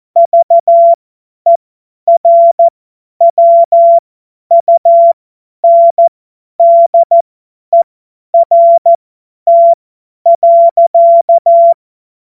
Morse code, heard as a keyed tone, about 14 wpm, spaced as for 10 wpm, 680 Hz. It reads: VERWUNDERT.
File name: Morsekurs / Witze / Witz 07. März 2023 50 F70